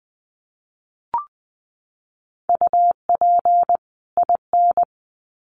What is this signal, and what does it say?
Morse code, heard as a keyed tone, about 20 wpm, 700 Hz. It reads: UP IN
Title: Sets of 2 Words 20wpm